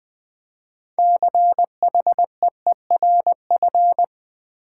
Morse code, heard as a keyed tone, about 20 wpm, 700 Hz. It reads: CHEERF
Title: Build A Word - 2 Letters At A Time - 8-Letter Words - 20wpm